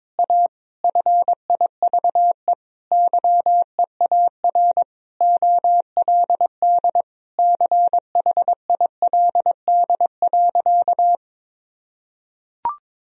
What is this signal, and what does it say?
Morse code, heard as a keyed tone, about 22 wpm, 700 Hz. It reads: A FIVE YEAR OLD CHILD.
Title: Sentences from Top 1000 Words - Review Part 2 - 22wpm